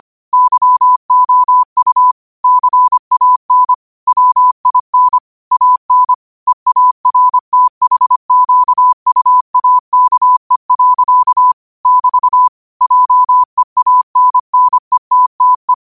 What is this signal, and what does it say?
Morse code, heard as a keyed tone, about 25 wpm, 1000 Hz. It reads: YOU CAN WIN AN EARTHQUAKE. <BT> JEANNETTE
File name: qotdfiles / 25wpm